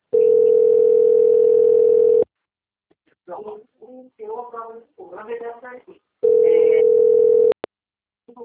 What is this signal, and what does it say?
Ringback tone (North America), 440 + 480 Hz, with speech.
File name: out- - - - - .67706